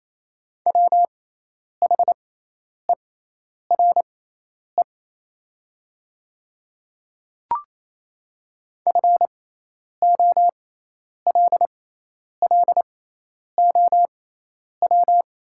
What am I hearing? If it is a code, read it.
Morse code, heard as a keyed tone, about 28 wpm, spaced as for 10 wpm, 700 Hz. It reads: WHERE FOLLOW
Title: 6x Character Spacing - Top 300 Words - 28wpm